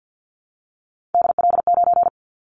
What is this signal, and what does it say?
Morse code, heard as a keyed tone, about 50 wpm, 700 Hz. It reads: DL9
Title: Callsign Prefixes for Top 25 DXCC - Repeated 3x - 50wpm